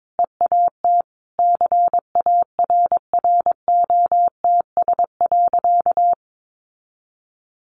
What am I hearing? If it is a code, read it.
Morse code, heard as a keyed tone, about 22 wpm, 700 Hz. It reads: EAT CARROTS.
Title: Easy Phrases 22wpm